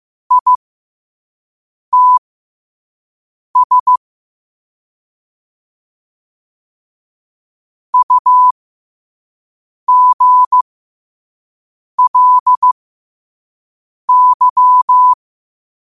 Morse code, heard as a keyed tone, about 15 wpm, spaced as for 5 wpm, 1000 Hz. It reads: ITS UGLY